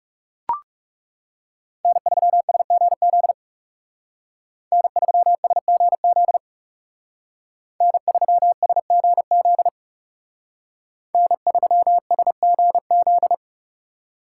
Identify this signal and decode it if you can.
Morse code, heard as a keyed tone, about 38 wpm, 700 Hz. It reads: N3SGZ N3SGZ N3SGZ N3SGZ